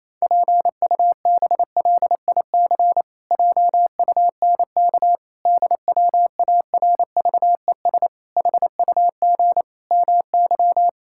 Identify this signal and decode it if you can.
Morse code, heard as a keyed tone, about 28 wpm, 700 Hz. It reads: PUBLIC JUNK DWARVES HUG MY
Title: Pangram Challenge v2 - 28wpm